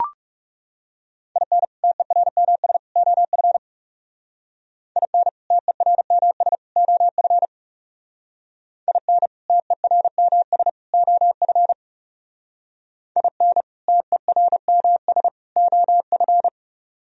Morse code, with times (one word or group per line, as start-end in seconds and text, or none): 1.4-1.6 IN
1.8-2.8 TERMS
3.0-3.6 OF
5.0-5.3 IN
5.5-6.5 TERMS
6.8-7.5 OF
8.9-9.3 IN
9.5-10.7 TERMS
10.9-11.7 OF
13.2-13.6 IN
13.9-15.3 TERMS
15.6-16.5 OF